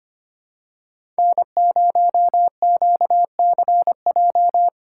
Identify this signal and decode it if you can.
Morse code, heard as a keyed tone, about 25 wpm, 700 Hz. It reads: N0QCJ
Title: USA Call Signs - 5 Characters - 4x Character Spacing - 25wpm